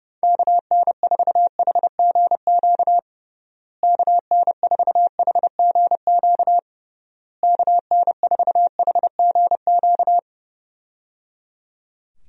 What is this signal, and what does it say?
Morse code, heard as a keyed tone, about 31 wpm, 700 Hz. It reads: KN4HGQ KN4HGQ KN4HGQ
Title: USA Call Signs - 6 Characters - Repeated 3x - 30wpm